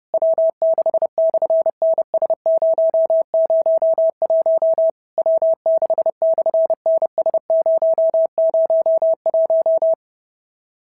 Morse code, heard as a keyed tone, about 30 wpm, 650 Hz. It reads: W6/NS001 W6/NS001
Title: SOTA QSO Practice 30wpm